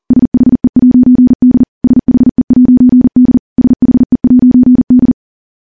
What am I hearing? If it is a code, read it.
Morse code, heard as a keyed tone, about 43 wpm, 250 Hz. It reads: SHE'D SHE'D SHE'D
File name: morse / morse-training-40WPM